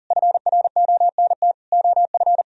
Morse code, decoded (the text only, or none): FRONT OF